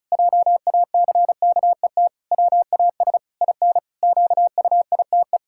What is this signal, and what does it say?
Morse code, heard as a keyed tone, about 36 wpm, 700 Hz. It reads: JACKET WAS IN QUITE
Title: Pangram Challenge v2 - 35wpm